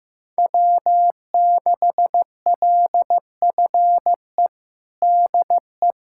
Morse code, heard as a keyed tone, about 15 wpm, 700 Hz. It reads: W6LFE DE
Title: USA Callsign DE Callsign - 5 Characters - 15wpm